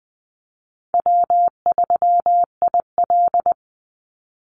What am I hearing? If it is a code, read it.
Morse code, heard as a keyed tone, about 19 wpm, 700 Hz. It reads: W3IL
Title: Easy Call Signs - 20wpm